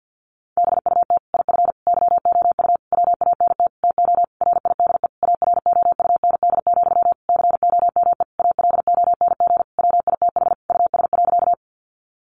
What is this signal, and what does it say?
Morse code, heard as a keyed tone, about 50 wpm, 700 Hz. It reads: BUT IF YOU WANT TO RIDE AROUND, COME ALONG WITH US.
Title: Sentences from Top 1200 Words 50wpm